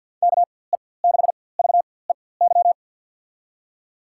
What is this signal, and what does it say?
Morse code, heard as a keyed tone, about 53 wpm, spaced as for 24 wpm, 700 Hz. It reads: KE6VEY
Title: USA Call Signs - 6 Characters - 4x Character Spacing - 50wpm